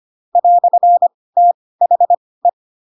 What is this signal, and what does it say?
Morse code, heard as a keyed tone, about 25 wpm, spaced as for 18 wpm, 700 Hz. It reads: "THE